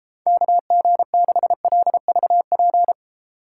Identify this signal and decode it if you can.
Morse code, heard as a keyed tone, about 38 wpm, 700 Hz. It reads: KG6LVP